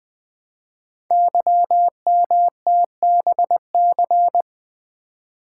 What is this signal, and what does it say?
Morse code, heard as a keyed tone, about 20 wpm, 700 Hz. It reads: YMTBC
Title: Code Groups - 5 Letters - 20wpm